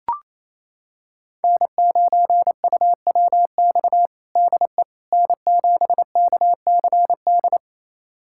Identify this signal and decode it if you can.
Morse code, heard as a keyed tone, about 28 wpm, 700 Hz. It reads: N9UWX DE N7KCD
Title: USA Callsign DE Callsign - 5 Characters - 28wpm